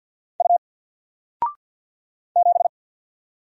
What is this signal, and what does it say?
Morse code, heard as a keyed tone, about 50 wpm, 700 Hz. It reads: U 7
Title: Single Letter-Number 50wpm